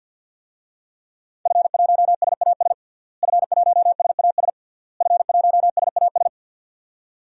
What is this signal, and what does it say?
Morse code, heard as a keyed tone, about 53 wpm, 700 Hz. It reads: F1SAS F1SAS F1SAS
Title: DXCC-100 Call Signs - 5 Characters - Repeated 3x - 50wpm